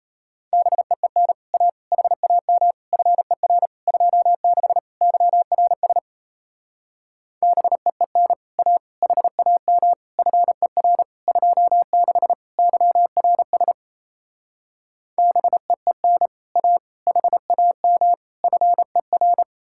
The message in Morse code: BEEN A HAM FER 26 YRS BEEN A HAM FER 26 YRS BEEN A HAM FER

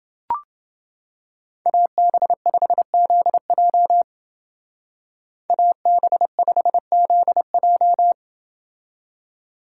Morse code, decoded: AB5ZJ AB5ZJ

